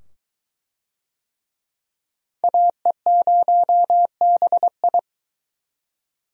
Morse code, heard as a keyed tone, about 21 wpm, 700 Hz. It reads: AE0BI